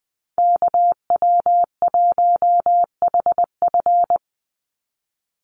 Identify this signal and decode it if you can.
Morse code, heard as a keyed tone, about 20 wpm, 700 Hz. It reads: KW1HF
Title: USA Call Signs 20wpm